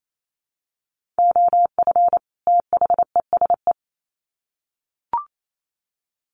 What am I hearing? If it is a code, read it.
Morse code, heard as a keyed tone, about 28 wpm, 700 Hz. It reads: OF THESE
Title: Sets of 2 Words 28wpm